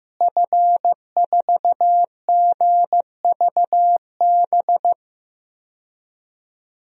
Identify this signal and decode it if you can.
Morse code, heard as a keyed tone, about 15 wpm, 700 Hz. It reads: F4GVB
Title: DXCC-100 Call Signs - 5 Characters - Repeated 3x - 15wpm